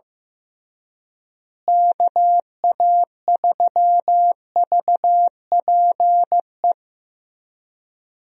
Morse code, heard as a keyed tone, about 15 wpm, 700 Hz. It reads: KA3VPE
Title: USA Call Signs - 6 Characters - Repeated 3x - 15wpm